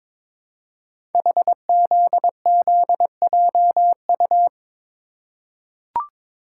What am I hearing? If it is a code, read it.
Morse code, heard as a keyed tone, about 22 wpm, 700 Hz. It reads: HZZJU
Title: Code Groups - 5 Letters - 22wpm